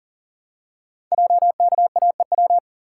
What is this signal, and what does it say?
Morse code, heard as a keyed tone, about 39 wpm, 690 Hz. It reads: JKAEW
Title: Code Groups - 5 Letters - 40wpm